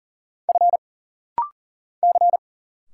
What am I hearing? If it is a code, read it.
Morse code, heard as a keyed tone, about 40 wpm, 700 Hz. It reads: F C
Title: Single Letters - Rapid-Fire 40wpm